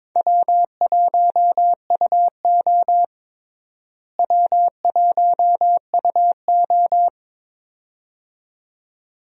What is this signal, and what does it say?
Morse code, heard as a keyed tone, about 22 wpm, 700 Hz. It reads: W1UO W1UO
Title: USA Call Signs - 4 Characters - Repeated 2x - 22wpm